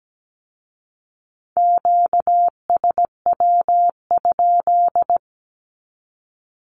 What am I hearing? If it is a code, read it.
Morse code, heard as a keyed tone, about 17 wpm, 700 Hz. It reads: QSW?
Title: Q-Codes with Spoken Meaning - 17wpm